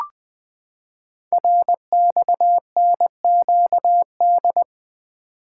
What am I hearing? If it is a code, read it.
Morse code, heard as a keyed tone, about 20 wpm, 700 Hz. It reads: RXNQD
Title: Code Groups - 5 Letters - 20wpm